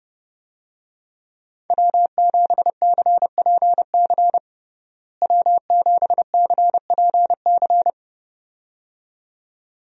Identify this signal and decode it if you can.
Morse code, heard as a keyed tone, about 30 wpm, 700 Hz. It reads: W7CPC W7CPC
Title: USA Call Signs - 5 Characters - Repeated 2x - 30wpm